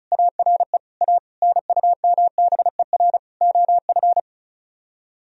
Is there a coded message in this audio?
Yes: Morse code from 0.1 to 4.2 s, heard as a keyed tone, about 45 wpm, 700 Hz. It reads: ARE A NUMBER OF